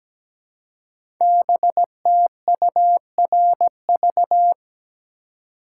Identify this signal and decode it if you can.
Morse code, heard as a keyed tone, about 17 wpm, 700 Hz. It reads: BTURV
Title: Code Groups - 5 Letters - 17wpm